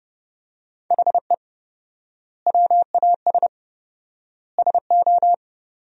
Morse code, heard as a keyed tone, about 30 wpm, 700 Hz. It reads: HE WAS SO
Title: 4x Word-Spacing - Sentences from Top 200 Words - 30wpm